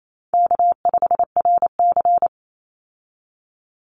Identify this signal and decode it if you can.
Morse code, heard as a keyed tone, about 28 wpm, 700 Hz. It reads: K5RC
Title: Easy Call Signs Repeated 2x - 28wpm